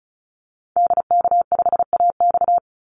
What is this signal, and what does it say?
Morse code, heard as a keyed tone, about 34 wpm, 700 Hz. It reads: DK5AX
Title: Call Signs Repeated 2x - 35wpm